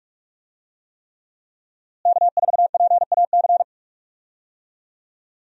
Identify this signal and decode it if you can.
Morse code, heard as a keyed tone, about 41 wpm, 700 Hz. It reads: K4PAC
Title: USA Call Signs - 5 Characters - Repeated 3x - 45wpm